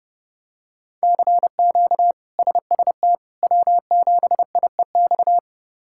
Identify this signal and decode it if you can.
Morse code, heard as a keyed tone, about 30 wpm, 700 Hz. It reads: CQ SST W7IEX
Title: K1USN SST QSO Practice - 30wpm